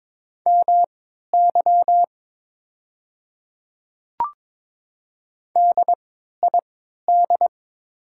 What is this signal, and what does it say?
Morse code, heard as a keyed tone, about 22 wpm, spaced as for 13 wpm, 700 Hz. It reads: MY DID
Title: 3x Character Spacing - Top 200 Words - 22wpm